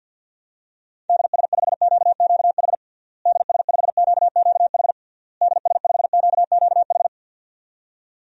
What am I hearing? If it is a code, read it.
Morse code, heard as a keyed tone, about 51 wpm, 700 Hz. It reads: DS5QQH DS5QQH DS5QQH
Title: DXCC-100 Call Signs - 6 Characters - Repeated 3x - 50wpm